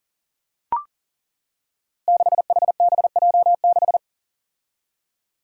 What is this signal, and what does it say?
Morse code, heard as a keyed tone, about 42 wpm, 700 Hz. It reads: 6HBJ6
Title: Five Challenging Letter-Number Combinations 40wpm